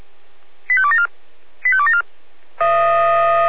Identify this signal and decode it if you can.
Five-tone selective call (ZVEI), heard as quick runs of five tones, tones of about 70 ms. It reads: 85274 85274